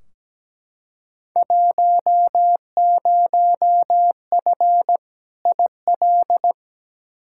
Morse code, heard as a keyed tone, about 17 wpm, 700 Hz. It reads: 10F IL
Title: Field Day Exchanges Encore - 17wpm